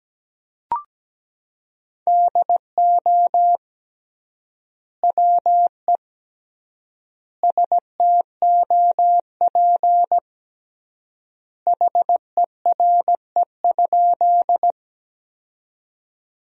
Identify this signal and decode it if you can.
Morse code, heard as a keyed tone, about 17 wpm, 700 Hz. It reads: DO WE STOP HERE?
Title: 3x Word-Spacing - Sentences from Top 300 Words - 17wpm